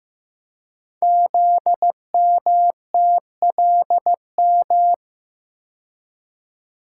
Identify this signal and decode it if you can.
Morse code, heard as a keyed tone, about 15 wpm, 700 Hz. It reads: ZMTLM
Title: Code Groups - 5 Letters - 15wpm